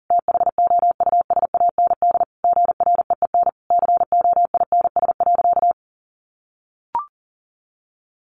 Morse code, heard as a keyed tone, about 40 wpm, 700 Hz. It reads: THOUSAND GREEN COINS.
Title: A Sound Of Thunder 40wpm